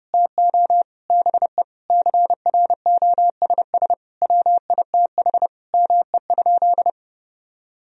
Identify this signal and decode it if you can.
Morse code, heard as a keyed tone, about 32 wpm, 700 Hz. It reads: TO BE CROSS WITH ME?
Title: Sentences from Top 1000 Words - Review Part 1 - 30wpm